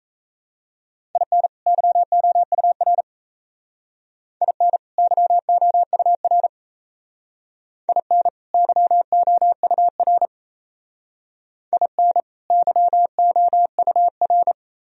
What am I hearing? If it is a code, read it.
Morse code, heard as a keyed tone, about 34 wpm, 700 Hz. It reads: IN YOUR IN YOUR IN YOUR IN YOUR
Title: Speed-Racing - Sets of 2 Words 28wpm